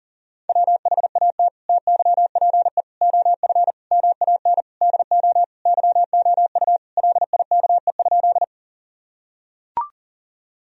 Morse code, heard as a keyed tone, about 40 wpm, 700 Hz. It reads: WHAT TYPE OF MAN DO YOU LIKE?
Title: Sentences from Top 1000 Words - Review Part 3 - 40wpm